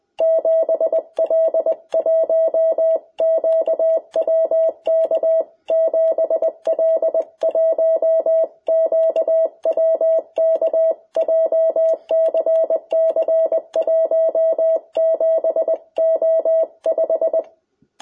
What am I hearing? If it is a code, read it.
Morse code, heard as a keyed tone, about 20 wpm, 660 Hz. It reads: 7L1QWK 7L1QWK JCC 17O5
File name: jh8cbh / 2021cw